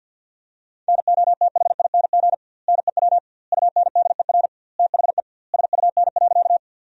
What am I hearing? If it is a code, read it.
Morse code, heard as a keyed tone, about 50 wpm, 700 Hz. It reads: NOTHING NEW UNDER THE SUN.
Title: 2x Word-Spacing - Sentences from Top 500 Words - 50wpm